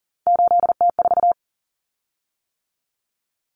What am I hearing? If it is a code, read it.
Morse code, heard as a keyed tone, about 40 wpm, 700 Hz. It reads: 8T4
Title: Common Cut Numbers - 3-Digit 40wpm